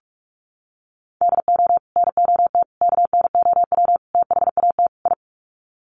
Morse code, heard as a keyed tone, about 44 wpm, 700 Hz. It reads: DO NOT KNOW THAT I
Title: Sets of 5 Words Encore2 45wpm